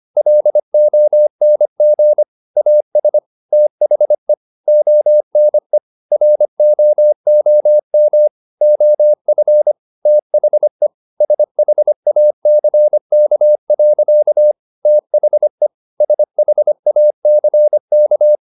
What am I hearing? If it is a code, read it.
Morse code, heard as a keyed tone, about 25 wpm, 600 Hz. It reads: LONG AS THE ONE ROOM OF THE SHACK. THE SHACK
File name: the old man and the sea part 1 25wpm